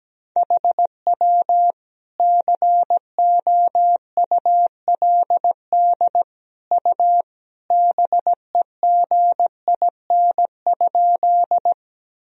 Morse code, heard as a keyed tone, about 17 wpm, 700 Hz. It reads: HW COULD U BEGIN?
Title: Sentences from Top 300 Words and Abbreviations 17wpm